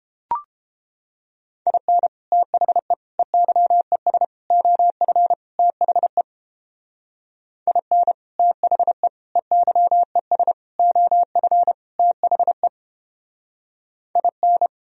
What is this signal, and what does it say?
Morse code, heard as a keyed tone, about 31 wpm, 700 Hz. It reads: IN THE EYES OF THE IN THE EYES OF THE IN